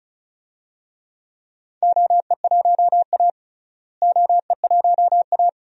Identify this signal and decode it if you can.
Morse code, heard as a keyed tone, about 34 wpm, 700 Hz. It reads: OE1A OE1A
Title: DXCC-100 Call Signs - 4 Characters - Repeated 2x - 35wpm